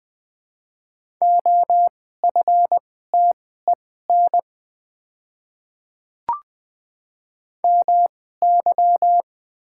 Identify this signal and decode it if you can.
Morse code, heard as a keyed tone, about 20 wpm, spaced as for 14 wpm, 700 Hz. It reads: OFTEN MY
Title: 2x Character Spacing - Top 500 Words - 20wpm